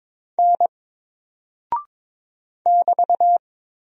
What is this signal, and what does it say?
Morse code, heard as a keyed tone, about 22 wpm, 700 Hz. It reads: N <BT>
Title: Single Character (Letter:Number:Punctuation:Prosign) 22wpm